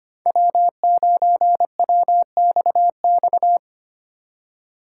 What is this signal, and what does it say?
Morse code, heard as a keyed tone, about 32 wpm, 700 Hz. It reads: W9WXX